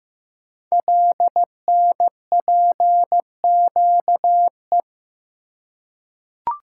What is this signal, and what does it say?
Morse code, heard as a keyed tone, about 15 wpm, 700 Hz. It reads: LNPQE
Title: Code Groups - 5 Letters - 15wpm